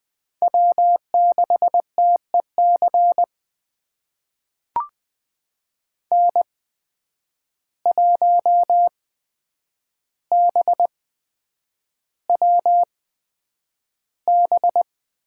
Morse code, heard as a keyed tone, about 20 wpm, spaced as for 9 wpm, 700 Hz. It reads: W6TEC N1BWB
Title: USA Call Signs - 5 Characters - 8x Character Spacing - 20wpm